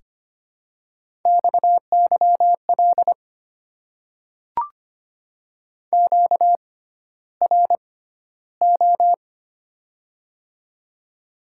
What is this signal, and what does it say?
Morse code, heard as a keyed tone, about 25 wpm, spaced as for 13 wpm, 700 Hz. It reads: XYL QRO
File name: Top 90 QSO Elements - 6x Character Spacing - 25wpm